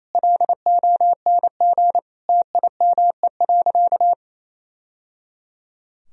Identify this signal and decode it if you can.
Morse code, heard as a keyed tone, about 28 wpm, 700 Hz. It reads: LONG TIME.
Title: Sentences from Top 1000 Words - Review Part 2 - 28wpm